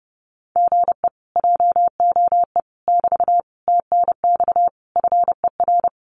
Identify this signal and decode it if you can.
Morse code, heard as a keyed tone, about 30 wpm, 700 Hz. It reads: GE JOE <BT> TNX FER
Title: Common QSO Phrases Encore 30wpm